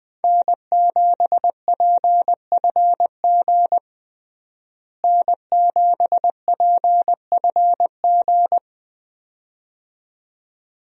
Morse code, heard as a keyed tone, about 20 wpm, 700 Hz. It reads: N7PFG N7PFG
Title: USA Call Signs - 5 Characters - Repeated 2x - 20wpm